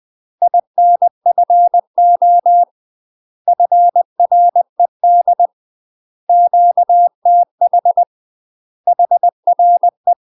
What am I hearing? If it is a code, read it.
Morse code, heard as a keyed tone, about 20 wpm, 700 Hz. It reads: INFO FRED QTH HRE
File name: UK MASTER QSOs TRAINING/20WPM